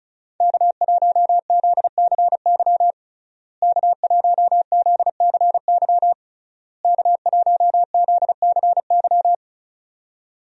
Morse code, heard as a keyed tone, about 36 wpm, 700 Hz. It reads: K1ZCY K1ZCY K1ZCY
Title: USA Call Signs - 5 Characters - Repeated 3x - 35wpm